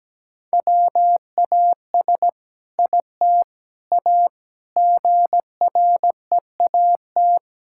Morse code, heard as a keyed tone, about 17 wpm, 700 Hz. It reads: WAS IT A GREAT
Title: Sentences from Top 1000 Words - Review Part 5 - 17wpm